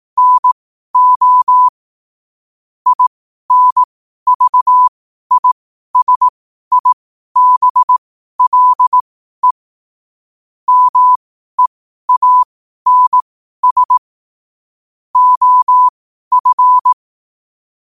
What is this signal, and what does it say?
Morse code, heard as a keyed tone, about 18 wpm, spaced as for 12 wpm, 1000 Hz. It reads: NO INVISIBLE MEANS OF
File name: qotdfiles / 12wpm